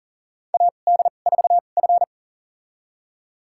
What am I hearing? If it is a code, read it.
Morse code, heard as a keyed tone, about 40 wpm, spaced as for 29 wpm, 700 Hz. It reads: AD4F
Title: USA Call Signs - 2x Character Spacing - 40wpm